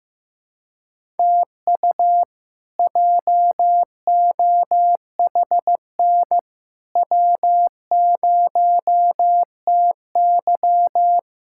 Morse code, heard as a keyed tone, about 15 wpm, 700 Hz. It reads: TU JOHN W0TY
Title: K1USN SST QSO Practice - 15wpm